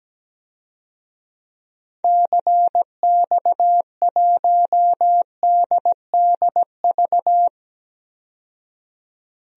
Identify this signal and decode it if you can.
Morse code, heard as a keyed tone, about 17 wpm, 700 Hz. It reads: CX1DDV
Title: DXCC-100 Call Signs - 6 Characters - 17wpm